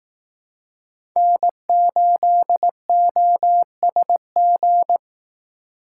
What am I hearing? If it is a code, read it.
Morse code, heard as a keyed tone, about 18 wpm, 700 Hz. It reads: N8OSG
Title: Speed Racing - USA Call Signs 15wpm